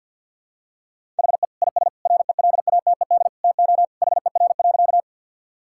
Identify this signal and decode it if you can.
Morse code, heard as a keyed tone, about 50 wpm, 700 Hz. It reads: HE IS RELATED TO HER.
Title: Sentences from Top 1000 Words - Review Part 5 - 50wpm